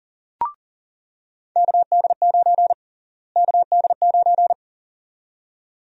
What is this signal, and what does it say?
Morse code, heard as a keyed tone, about 40 wpm, 700 Hz. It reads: KD9 KD9
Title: USA Call Sign Prefixes - Repeated 2x - 40wpm